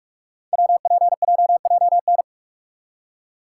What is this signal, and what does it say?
Morse code, heard as a keyed tone, about 44 wpm, 700 Hz. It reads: WPJJN
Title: Code Groups - 5 Letters - 45wpm